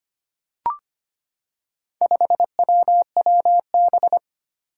Morse code, heard as a keyed tone, about 25 wpm, 700 Hz. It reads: 5WWB
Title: Four Challenging Letter-Number Combinations 25wpm